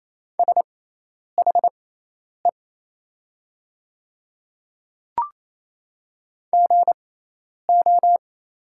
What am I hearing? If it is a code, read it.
Morse code, heard as a keyed tone, about 27 wpm, spaced as for 10 wpm, 700 Hz. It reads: SHE GO
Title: 6x Character Spacing - Top 100 Words - 28wpm